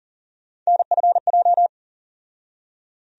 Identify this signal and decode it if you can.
Morse code, heard as a keyed tone, about 40 wpm, 700 Hz. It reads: NFJ
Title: USA Call Sign Postfixes - 3 Letters 40wpm